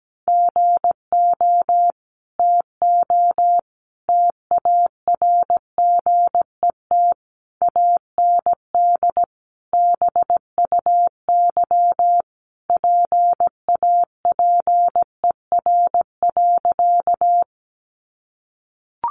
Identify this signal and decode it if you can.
Morse code, heard as a keyed tone, about 17 wpm, 700 Hz. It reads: GO TO TARGET AND BUY PAPER.
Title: Sentences from Top 1200 Words 17wpm